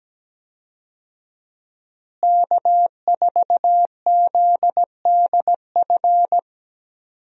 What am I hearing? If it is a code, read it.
Morse code, heard as a keyed tone, about 17 wpm, 700 Hz. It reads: K4ZDF
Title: USA Call Signs - 5 Characters - Repeated 3x - 17wpm